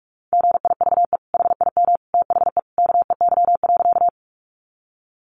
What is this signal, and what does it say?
Morse code, heard as a keyed tone, about 44 wpm, 700 Hz. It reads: GIVE HIM THE KEY.
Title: Sentences from Top 600 Words 45wpm